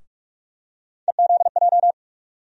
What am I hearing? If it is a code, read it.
Morse code, heard as a keyed tone, about 43 wpm, 700 Hz. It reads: EZJ
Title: USA Call Sign Postfixes - 3 Letters 45wpm